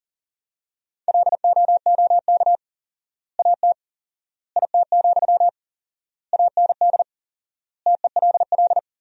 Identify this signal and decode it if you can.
Morse code, heard as a keyed tone, about 40 wpm, 700 Hz. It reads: LOOK AT IT, AND TELL
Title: 4x Word-Spacing - Sentences from Top 200 Words - 40wpm